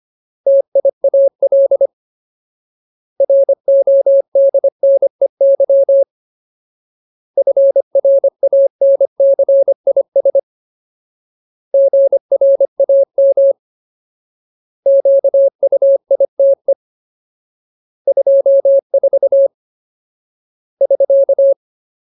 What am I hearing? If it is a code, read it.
Morse code, heard as a keyed tone, about 25 wpm, 550 Hz. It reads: TIAL RODNEY FRANCIS GRAM QUITE 24 <SK>
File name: Ultimate Word List / 25WPM Ultimate